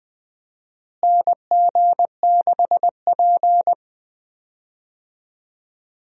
Morse code, heard as a keyed tone, about 20 wpm, 700 Hz. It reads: NG6P
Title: USA Call Signs - 4 Characters - Repeated 3x - 20wpm